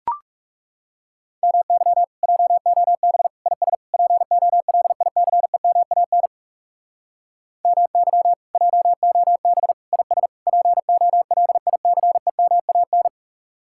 Morse code, heard as a keyed tone, about 36 wpm, 700 Hz. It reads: MY JOB IS POLICEMAN MY JOB IS POLICEMAN
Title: Speed-Racing Common QSO Phrases - 30wpm